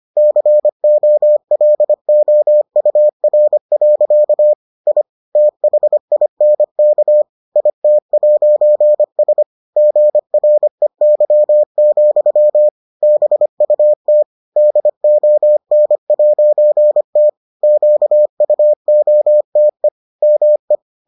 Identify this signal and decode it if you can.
Morse code, heard as a keyed tone, about 25 wpm, 600 Hz. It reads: COLOUR. I THINK IT'S GREY, BUT DON'T QUOTE ME